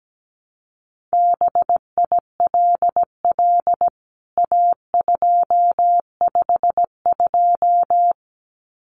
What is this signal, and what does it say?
Morse code, heard as a keyed tone, about 17 wpm, 700 Hz. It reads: BILL A252
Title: CWT Exchange 17wpm